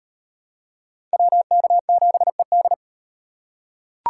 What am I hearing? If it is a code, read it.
Morse code, heard as a keyed tone, about 38 wpm, 700 Hz. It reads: WK7ED